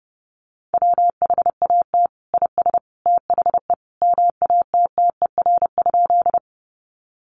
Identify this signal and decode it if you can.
Morse code, heard as a keyed tone, about 30 wpm, 700 Hz. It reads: WHAT IS THE MATTER?
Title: Sentences from Top 700 Words 30wpm